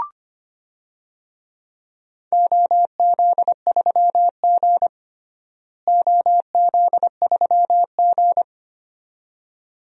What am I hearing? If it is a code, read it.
Morse code, heard as a keyed tone, about 25 wpm, 700 Hz. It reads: OZ3G OZ3G
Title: DXCC-100 Call Signs - 4 Characters - Repeated 2x - 25wpm